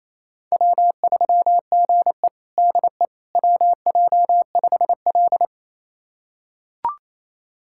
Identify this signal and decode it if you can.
Morse code, heard as a keyed tone, about 28 wpm, 700 Hz. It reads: W3GE DE WJ5L
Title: USA Callsign DE Callsign - 4 Characters - 28wpm